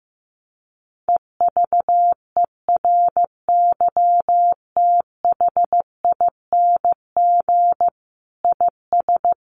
Morse code, heard as a keyed tone, about 15 wpm, 700 Hz. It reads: EVERYTHING IS
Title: Sentences from Top 700 Words 15wpm